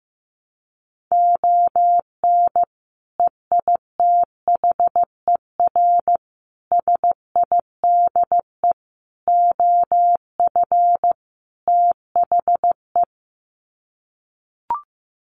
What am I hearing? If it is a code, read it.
Morse code, heard as a keyed tone, about 15 wpm, 700 Hz. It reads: ON EITHER SIDE OF THE
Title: Sets of 5 Words 15wpm